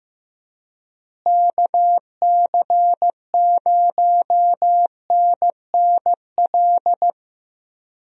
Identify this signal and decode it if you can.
Morse code, heard as a keyed tone, about 15 wpm, 700 Hz. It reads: KC0NNL
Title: USA Call Signs - 6 Characters - Repeated 3x - 15wpm